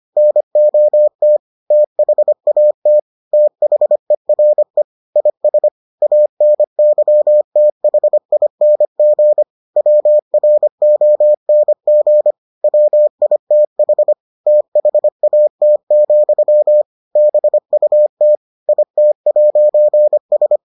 Morse code, heard as a keyed tone, about 25 wpm, 600 Hz. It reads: NOT THAT THERE IS ANYTHING WRONG WITH THAT, BUT IT'S